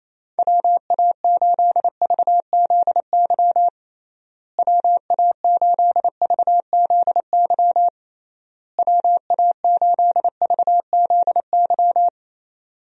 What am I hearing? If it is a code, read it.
Morse code, heard as a keyed tone, about 28 wpm, 700 Hz. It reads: WA8VZY WA8VZY WA8VZY
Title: USA Call Signs - 6 Characters - Repeated 3x - 28wpm